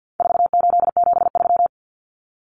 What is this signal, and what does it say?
Morse code, heard as a keyed tone, about 50 wpm, 700 Hz. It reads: 4873
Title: Common Cut Numbers - 4-Digit 50wpm